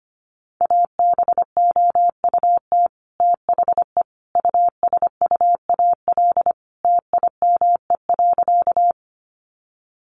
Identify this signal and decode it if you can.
Morse code, heard as a keyed tone, about 25 wpm, 700 Hz. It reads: ABOUT THE USUAL TIME.
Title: Sentences from Top 400 Words 25wpm